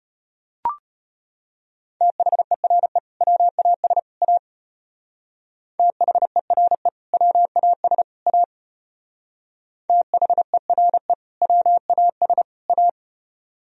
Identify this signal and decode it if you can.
Morse code, heard as a keyed tone, about 32 wpm, 700 Hz. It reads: THERE WAS A THERE WAS A THERE WAS A